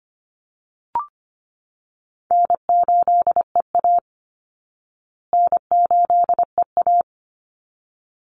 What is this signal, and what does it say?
Morse code, heard as a keyed tone, about 25 wpm, 700 Hz. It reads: N8EA N8EA
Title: Call Signs Repeated 3x - 25wpm